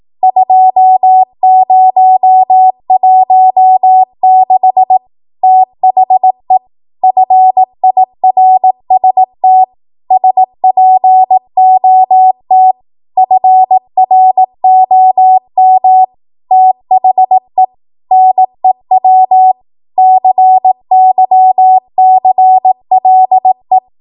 Morse code, heard as a keyed tone, about 18 wpm, 750 Hz. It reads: 2016 THE FIRST SPOT FROM THE NEW CYCLE
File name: Archive / 18 WPM / 18WPM